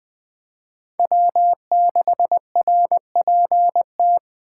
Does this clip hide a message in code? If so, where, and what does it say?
Yes: Morse code from 1.0 to 4.2 s, heard as a keyed tone, about 21 wpm, 700 Hz. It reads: W6RPT